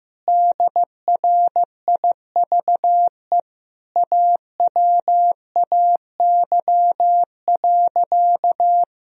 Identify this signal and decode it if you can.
Morse code, heard as a keyed tone, about 15 wpm, 700 Hz. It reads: DRIVE AWAY.
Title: Sentences from Top 1000 Words - Review Part 3 - 15wpm